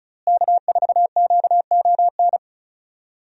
Morse code, heard as a keyed tone, about 36 wpm, 700 Hz. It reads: K4QON
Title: USA Call Signs - 5 Characters - Repeated 3x - 35wpm